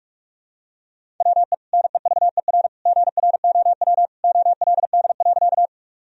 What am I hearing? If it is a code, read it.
Morse code, heard as a keyed tone, about 47 wpm, 700 Hz. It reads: WE NEVER GROW OLD.
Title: Sentences from Top 1000 Words - Review Part 1 - 45wpm